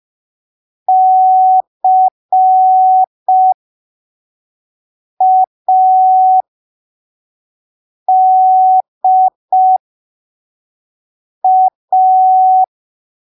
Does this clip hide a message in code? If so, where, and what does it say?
Yes: Morse code from 0.9 to 12.6 s, heard as a keyed tone, about 5 wpm, spaced as for 3 wpm, 750 Hz. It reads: CADA